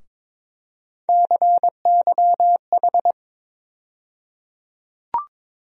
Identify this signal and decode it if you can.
Morse code, heard as a keyed tone, about 22 wpm, 700 Hz. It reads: CYH
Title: USA Call Sign Postfixes - 3 Letters 22wpm